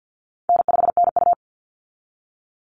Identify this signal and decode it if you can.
Morse code, heard as a keyed tone, about 50 wpm, 700 Hz. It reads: N5NU
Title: Easy Call Signs Repeated 2x - 50wpm